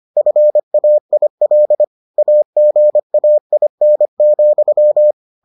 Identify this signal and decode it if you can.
Morse code, heard as a keyed tone, about 25 wpm, 600 Hz. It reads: FAIL AGAIN,